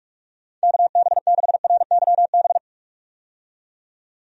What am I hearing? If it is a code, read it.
Morse code, heard as a keyed tone, about 46 wpm, 700 Hz. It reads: KB6RYB
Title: USA Call Signs - 6 Characters - 45wpm